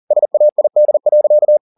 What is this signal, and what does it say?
Morse code, heard as a keyed tone, about 40 wpm, 600 Hz. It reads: SAID.